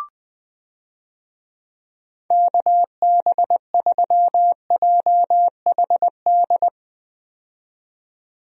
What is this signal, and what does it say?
Morse code, heard as a keyed tone, about 21 wpm, 700 Hz. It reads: KB3JHD
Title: USA Call Signs - 6 Characters - 20wpm